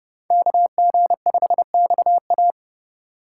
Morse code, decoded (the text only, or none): KG5XA